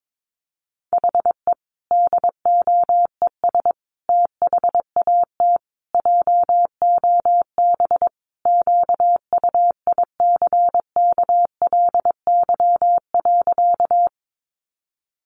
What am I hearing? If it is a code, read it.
Morse code, heard as a keyed tone, about 22 wpm, 700 Hz. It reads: HE DOES THAT JOB QUICKLY.
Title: Sentences from Top 700 Words 22wpm